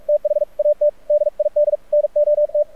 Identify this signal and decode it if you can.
Morse code, heard as a keyed tone, about 44 wpm, 590 Hz. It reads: THAT DID NOT
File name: archived / CRSnet